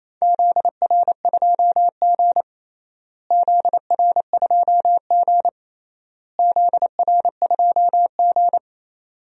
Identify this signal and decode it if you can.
Morse code, heard as a keyed tone, about 28 wpm, 700 Hz. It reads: ZR2G ZR2G ZR2G
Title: DXCC-100 Call Signs - 4 Characters - Repeated 3x - 28wpm